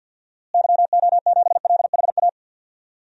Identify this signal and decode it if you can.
Morse code, heard as a keyed tone, about 48 wpm, 700 Hz. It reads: YO7LHA
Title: DXCC-100 Call Signs - 6 Characters - Repeated 3x - 50wpm